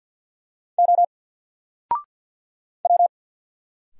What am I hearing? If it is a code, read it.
Morse code, heard as a keyed tone, about 50 wpm, 700 Hz. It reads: O W
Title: Single Letter-Number 50wpm